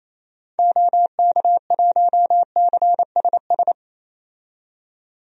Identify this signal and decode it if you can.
Morse code, heard as a keyed tone, about 28 wpm, 700 Hz. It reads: OK1CSS
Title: DXCC-100 Call Signs - 6 Characters - 28wpm